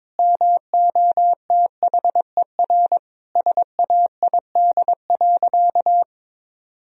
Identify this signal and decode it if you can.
Morse code, heard as a keyed tone, about 22 wpm, 700 Hz. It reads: MOTHER SAID.
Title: Sentences from Top 1000 Words - Review Part 4 - 22wpm